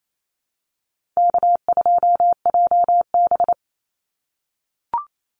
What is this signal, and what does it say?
Morse code, heard as a keyed tone, about 28 wpm, 700 Hz. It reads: K2JB
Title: NA SOTA Chaser Callsigns 28wpm